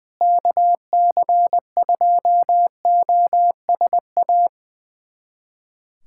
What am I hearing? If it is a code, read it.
Morse code, heard as a keyed tone, about 21 wpm, 700 Hz. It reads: KC2OSA
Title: USA Callsign DE Callsign 20wpm